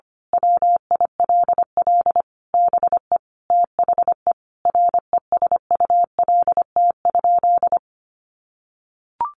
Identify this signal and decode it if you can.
Morse code, heard as a keyed tone, about 25 wpm, 700 Hz. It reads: WILL BE THE RESULT?
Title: Sentences from Top 700 Words 25wpm